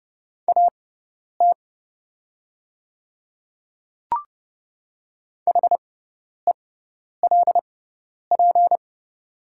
Morse code, heard as a keyed tone, about 30 wpm, spaced as for 10 wpm, 690 Hz. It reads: AT HELP
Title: 6x Character Spacing - Top 200 Words - 30wpm